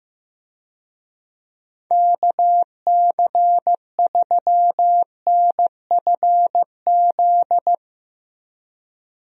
Morse code, heard as a keyed tone, about 15 wpm, 700 Hz. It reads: KC3NFZ
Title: USA Call Signs - 6 Characters - Repeated 3x - 15wpm